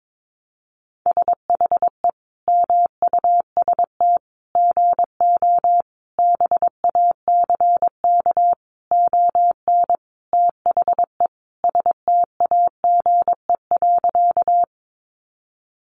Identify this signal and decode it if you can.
Morse code, heard as a keyed tone, about 22 wpm, 700 Hz. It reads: SHE MUST GO BACK ON THE STAGE.
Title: Sentences from Top 1200 Words 22wpm